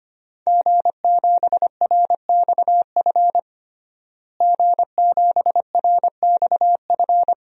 Morse code, heard as a keyed tone, about 25 wpm, 700 Hz. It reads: G7RXF G7RXF
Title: DXCC-100 Call Signs - 5 Characters - Repeated 2x - 25wpm